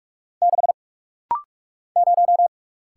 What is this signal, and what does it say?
Morse code, heard as a keyed tone, about 44 wpm, 700 Hz. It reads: 6 0
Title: Single Letter-Number - Rapid-Fire 45wpm